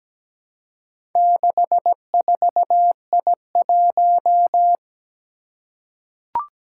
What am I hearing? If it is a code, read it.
Morse code, heard as a keyed tone, about 17 wpm, 700 Hz. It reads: 64I1
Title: Four Challenging Letter-Number Combinations 17wpm